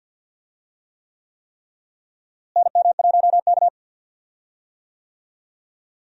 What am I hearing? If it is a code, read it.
Morse code, heard as a keyed tone, about 52 wpm, 700 Hz. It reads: NM1K
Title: USA Call Signs - 4 Characters - Repeated 3x - 50wpm